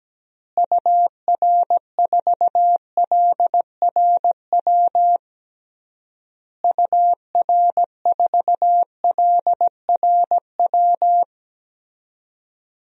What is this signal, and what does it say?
Morse code, heard as a keyed tone, about 17 wpm, 700 Hz. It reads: UR4LRW UR4LRW
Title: DXCC-100 Call Signs - 6 Characters - Repeated 3x - 17wpm